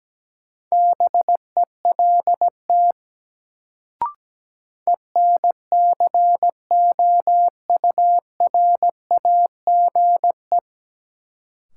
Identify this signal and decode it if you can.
Morse code, heard as a keyed tone, about 17 wpm, 700 Hz. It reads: BELT ENCOURAGE
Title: Sentences from Top 1400 Words - New Word Drill - 17wpm